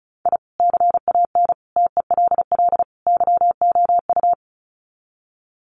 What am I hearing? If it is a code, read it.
Morse code, heard as a keyed tone, about 34 wpm, 700 Hz. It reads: I CAN TELL YOU
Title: Sets of 4 Words 35wpm